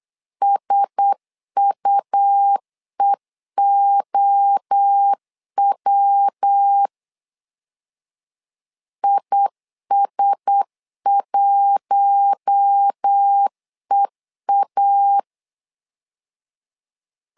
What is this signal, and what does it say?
Morse code, heard as a keyed tone, about 8 wpm, 800 Hz. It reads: SUEOW IS1EA
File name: LEZIONI / 4 Q1S 8wpm 2710